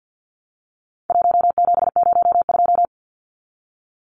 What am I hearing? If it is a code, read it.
Morse code, heard as a keyed tone, about 50 wpm, 700 Hz. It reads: 1702
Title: Numbers - 4-Digit 50wpm